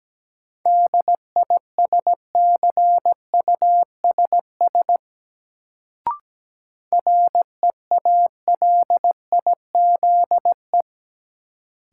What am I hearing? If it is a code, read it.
Morse code, heard as a keyed tone, about 17 wpm, 700 Hz. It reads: DISCUSS REALIZE
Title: Sentences from Top 1000 Words - New Word Drill - 17wpm